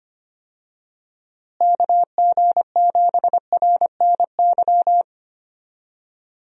Morse code, heard as a keyed tone, about 25 wpm, 700 Hz. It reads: KG7RNY
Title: USA Call Signs - 6 Characters - Repeated 3x - 25wpm